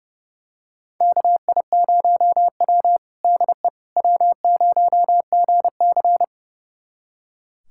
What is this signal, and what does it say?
Morse code, heard as a keyed tone, about 31 wpm, 700 Hz. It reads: KI0W DE W0GC
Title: USA Callsign DE Callsign - 4 Characters - 30wpm